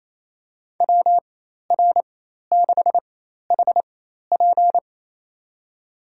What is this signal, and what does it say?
Morse code, heard as a keyed tone, about 28 wpm, spaced as for 13 wpm, 700 Hz. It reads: WR6HP
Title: USA Call Signs - 5 Characters - 4x Character Spacing - 28wpm